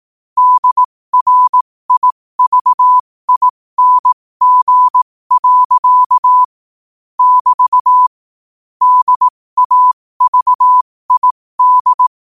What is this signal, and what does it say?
Morse code, heard as a keyed tone, about 18 wpm, spaced as for 15 wpm, 1000 Hz. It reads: DRIVING. <BT> DAVID